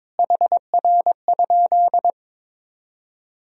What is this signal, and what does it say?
Morse code, heard as a keyed tone, about 22 wpm, 700 Hz. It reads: HR?
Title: Sentences from Top 300 Words and Abbreviations 22wpm